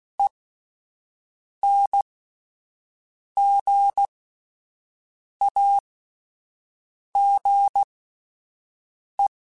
Morse code, heard as a keyed tone, about 16 wpm, spaced as for 5 wpm, 800 Hz. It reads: ENGAGE